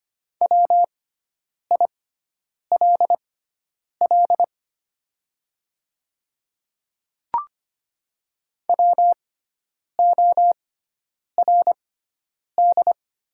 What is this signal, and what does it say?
Morse code, heard as a keyed tone, about 25 wpm, spaced as for 9 wpm, 700 Hz. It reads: WILL WORD
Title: 6x Character Spacing - Top 300 Words - 25wpm